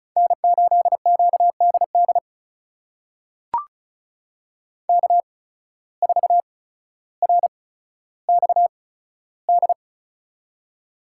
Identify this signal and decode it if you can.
Morse code, heard as a keyed tone, about 34 wpm, spaced as for 15 wpm, 700 Hz. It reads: N8QDD K4RXD
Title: USA Call Signs - 5 Characters - 8x Character Spacing - 35wpm